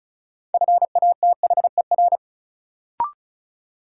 Morse code, heard as a keyed tone, about 36 wpm, 700 Hz. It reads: FATHER
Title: Sentences from Top 300 Words - New Word Drill - 35wpm